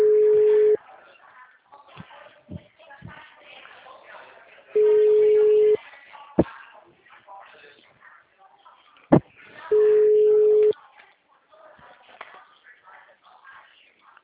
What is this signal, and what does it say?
Ringback tone (Europe, 425 Hz), with speech.
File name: out- -2012- - - .2609